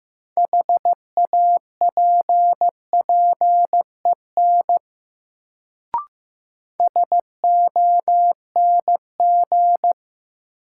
Morse code, heard as a keyed tone, about 15 wpm, 700 Hz. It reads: HAPPEN SONG
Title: Sentences from Top 400 Words - New Word Drill - 15wpm